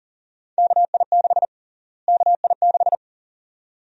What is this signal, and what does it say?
Morse code, heard as a keyed tone, about 40 wpm, 700 Hz. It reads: KI6 KI6
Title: USA Call Sign Prefixes - Repeated 2x - 40wpm